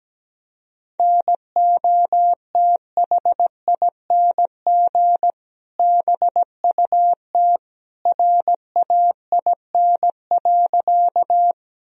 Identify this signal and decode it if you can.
Morse code, heard as a keyed tone, about 17 wpm, 700 Hz. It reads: NOTHING BUT RAIN.
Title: Sentences from Top 1000 Words - Review Part 1 - 17wpm